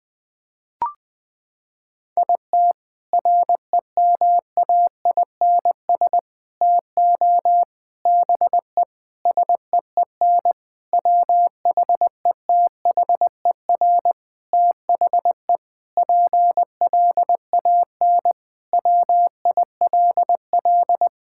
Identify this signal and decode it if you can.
Morse code, heard as a keyed tone, about 21 wpm, 700 Hz. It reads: IT REMAINS TO BE SEEN WHETHER THE PLAN WILL
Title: Sentences with 6-Word Phrases - 20wpm